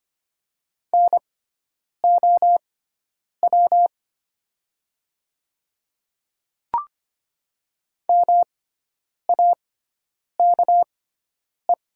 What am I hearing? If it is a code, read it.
Morse code, heard as a keyed tone, about 25 wpm, spaced as for 9 wpm, 690 Hz. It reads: NOW MAKE
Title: 6x Character Spacing - Top 200 Words - 25wpm